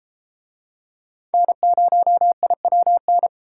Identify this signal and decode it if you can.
Morse code, heard as a keyed tone, about 33 wpm, 700 Hz. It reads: N0IWN